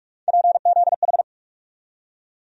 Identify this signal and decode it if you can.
Morse code, heard as a keyed tone, about 42 wpm, 700 Hz. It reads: PZH
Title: USA Call Sign Postfixes - 3 Letters 45wpm